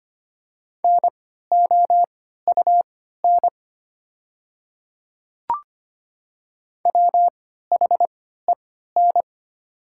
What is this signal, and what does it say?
Morse code, heard as a keyed tone, about 25 wpm, spaced as for 14 wpm, 700 Hz. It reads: NOUN WHEN
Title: 3x Character Spacing - Top 500 Words - 25wpm